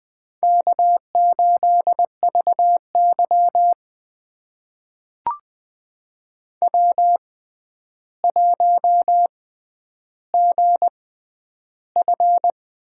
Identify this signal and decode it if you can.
Morse code, heard as a keyed tone, about 20 wpm, spaced as for 10 wpm, 700 Hz. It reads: K8VY W1GF
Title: USA Call Signs - 4 Characters - 6x Character Spacing - 20wpm